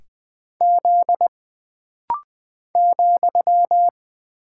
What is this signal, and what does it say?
Morse code, heard as a keyed tone, about 20 wpm, 700 Hz. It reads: Z ,
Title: Single Character (Letter:Number:Punctuation:Prosign) - Rapid-Fire - 20wpm